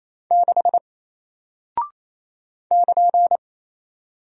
Morse code, heard as a keyed tone, about 28 wpm, 700 Hz. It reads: 6 <KN>
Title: Single Character (Letter:Number:Punctuation:Prosign) Alt 28wpm